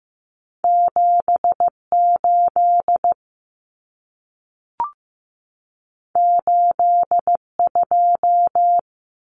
Morse code, heard as a keyed tone, about 15 wpm, 700 Hz. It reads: 78 82
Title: Numbers - 2-Digit 15wpm